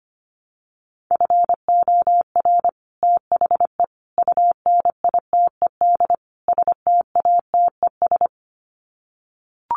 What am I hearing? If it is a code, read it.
Morse code, heard as a keyed tone, about 25 wpm, 700 Hz. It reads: FOR THE UNITED STATES
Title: Sets of 4 Words 25wpm